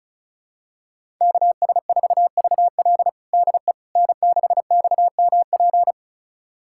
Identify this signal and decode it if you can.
Morse code, heard as a keyed tone, about 36 wpm, 700 Hz. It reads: KS4VL DE N6XMP
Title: USA Callsign DE Callsign - 5 Characters - 35wpm